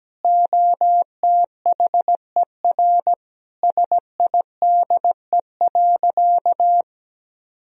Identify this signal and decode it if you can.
Morse code, heard as a keyed tone, about 17 wpm, 700 Hz. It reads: OTHER SIDE.
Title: Sentences from Top 1000 Words - Review Part 1 - 17wpm